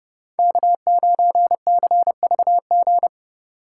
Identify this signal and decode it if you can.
Morse code, heard as a keyed tone, about 31 wpm, 700 Hz. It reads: K9CVG